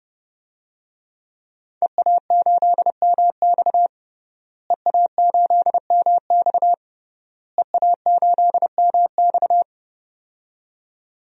Morse code, heard as a keyed tone, about 30 wpm, 700 Hz. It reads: EA8MX EA8MX EA8MX
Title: DXCC-100 Call Signs - 5 Characters - Repeated 3x - 30wpm